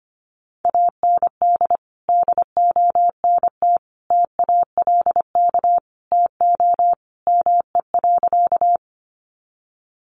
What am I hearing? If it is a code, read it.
Morse code, heard as a keyed tone, about 25 wpm, 700 Hz. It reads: AND DONT TALK TO ME.
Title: Sentences from Top 400 Words 25wpm